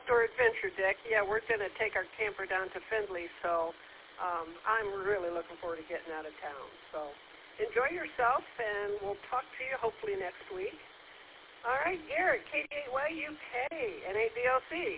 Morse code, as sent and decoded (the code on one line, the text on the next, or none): none